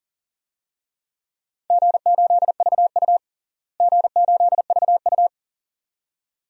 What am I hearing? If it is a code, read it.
Morse code, heard as a keyed tone, about 40 wpm, 700 Hz. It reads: G8VU G8VU
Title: DXCC-100 Call Signs - 4 Characters - Repeated 2x - 40wpm